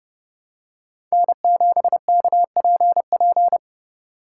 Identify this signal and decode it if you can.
Morse code, heard as a keyed tone, about 30 wpm, 700 Hz. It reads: N7KPP
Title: USA Call Signs - 8x Character Spacing - 30wpm